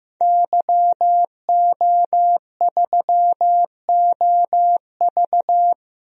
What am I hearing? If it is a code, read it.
Morse code, heard as a keyed tone, about 15 wpm, 700 Hz. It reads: YO3OV
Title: DXCC-100 Call Signs - 5 Characters - Repeated 3x - 15wpm